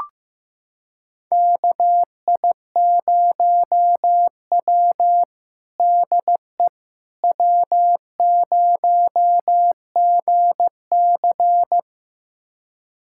Morse code, heard as a keyed tone, about 15 wpm, 700 Hz. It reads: KI0W DE W0GC
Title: USA Callsign DE Callsign - 4 Characters - 15wpm